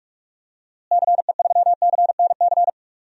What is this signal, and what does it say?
Morse code, heard as a keyed tone, about 46 wpm, 700 Hz. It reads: CE3CNC
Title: DXCC-100 Call Signs - 6 Characters - Repeated 3x - 45wpm